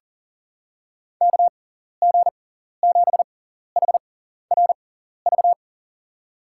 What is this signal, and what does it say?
Morse code, heard as a keyed tone, about 37 wpm, spaced as for 14 wpm, 700 Hz. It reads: KG7HRV
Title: USA Call Signs - 6x Character Spacing - 40wpm